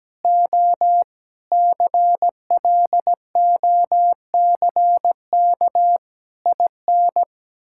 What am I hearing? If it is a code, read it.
Morse code, heard as a keyed tone, about 17 wpm, 700 Hz. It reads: O CLOCK IN
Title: Pangram Challenge v2 - 17wpm